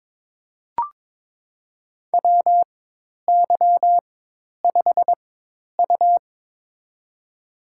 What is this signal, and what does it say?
Morse code, heard as a keyed tone, about 22 wpm, spaced as for 10 wpm, 700 Hz. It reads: WY5U
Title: USA Call Signs - 4x Character Spacing - 22wpm